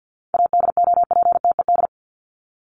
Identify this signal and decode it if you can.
Morse code, heard as a keyed tone, about 50 wpm, 700 Hz. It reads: ADOPTED
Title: ED Suffix - Top 100 Words with Base Words 50wpm